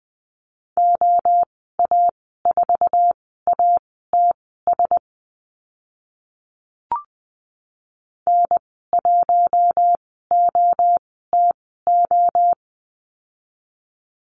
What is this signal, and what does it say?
Morse code, heard as a keyed tone, about 20 wpm, spaced as for 15 wpm, 700 Hz. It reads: OA4ATS N1OTO
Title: 069 Lesson 12g - Letter d - Call Signs 20wpm